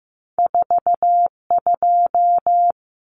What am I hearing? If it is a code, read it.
Morse code, heard as a keyed tone, about 15 wpm, 700 Hz. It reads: 42
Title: Numbers - 2-Digit 15wpm